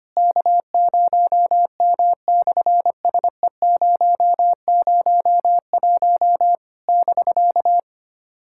Morse code, heard as a keyed tone, about 25 wpm, 700 Hz. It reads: K0M/SE001 <BK>
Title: SOTA QSO Practice 25wpm